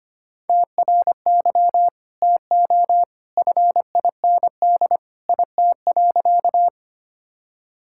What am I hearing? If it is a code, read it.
Morse code, heard as a keyed tone, about 25 wpm, 700 Hz. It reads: TRY TO FIND IT.
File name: practice / 25wpm / Sentences with 2-Word Phrases - 25wpm